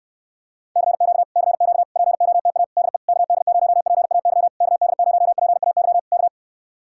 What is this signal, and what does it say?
Morse code, heard as a keyed tone, about 59 wpm, 700 Hz. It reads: CQ CQ POTA DE KD0PAY KD0PAY K